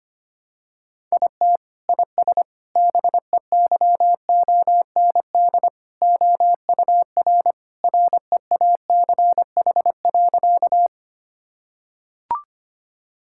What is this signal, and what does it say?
Morse code, heard as a keyed tone, about 25 wpm, 700 Hz. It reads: IT IS BEYOND OUR REACH.
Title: Sentences from Top 1000 Words - Review Part 6 - 25wpm